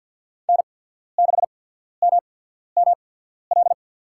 Morse code, heard as a keyed tone, about 54 wpm, spaced as for 14 wpm, 700 Hz. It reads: N6MML
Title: USA Call Signs - 5 Characters - 8x Character Spacing - 50wpm